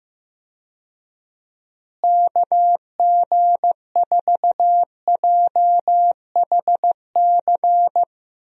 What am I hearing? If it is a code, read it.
Morse code, heard as a keyed tone, about 15 wpm, 700 Hz. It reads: KG4JHC
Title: USA Call Signs - 6 Characters - Repeated 2x - 15wpm